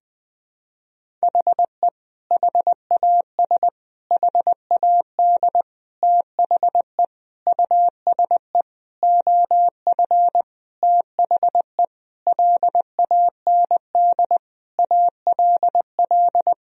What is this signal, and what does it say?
Morse code, heard as a keyed tone, about 20 wpm, 700 Hz. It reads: HE HAS HAD THE USE OF THE LAND ALL
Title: Sentences from Top 1000 Words - Review Part 1 - 20wpm